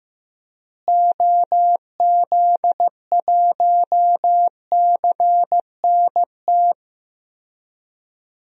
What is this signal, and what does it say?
Morse code, heard as a keyed tone, about 15 wpm, 700 Hz. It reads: OZ1CNT
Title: DXCC-100 Call Signs - 6 Characters - Repeated 2x - 15wpm